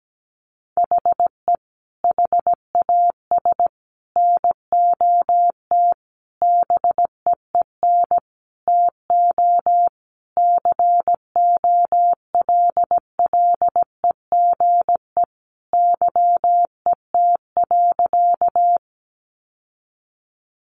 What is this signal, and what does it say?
Morse code, heard as a keyed tone, about 17 wpm, 700 Hz. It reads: HE HAS NOT BEEN TO COLLEGE YET.
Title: Sentences from Top 1300 Words 17wpm